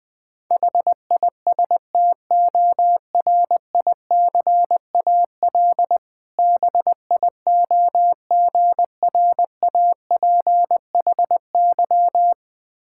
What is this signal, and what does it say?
Morse code, heard as a keyed tone, about 20 wpm, 700 Hz. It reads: HISTORICAL BIOGRAPHY
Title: Difficult Words with Hints - 20wpm